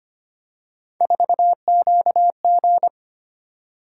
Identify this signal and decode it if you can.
Morse code, heard as a keyed tone, about 25 wpm, 700 Hz. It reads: 4QG
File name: Three Letter-Number Combinations - 25wpm